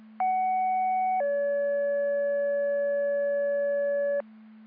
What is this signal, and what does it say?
Two-tone page, heard as a one-second tone, then a three-second tone, about 750 then 570 Hz, with noise. It is heard from 0.2 to 4.2 s.